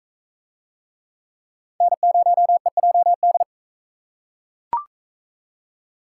Morse code, heard as a keyed tone, about 41 wpm, 700 Hz. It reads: N0EJD